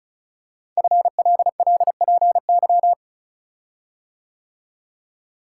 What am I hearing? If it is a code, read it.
Morse code, heard as a keyed tone, about 36 wpm, 690 Hz. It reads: FLLPY
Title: Code Groups - 5 Letters - 35wpm